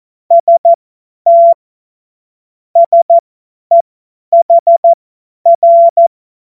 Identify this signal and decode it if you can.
Morse code, heard as a keyed tone, about 14 wpm, spaced as for 10 wpm, 680 Hz. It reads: ST SEHR
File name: Morsekurs / Witze / Witz 16 Januar 2024 50 F70